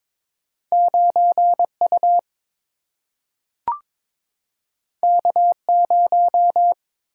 Morse code, heard as a keyed tone, about 22 wpm, 700 Hz. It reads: 9 U K 0
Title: Single-Letter Single-Number 22wpm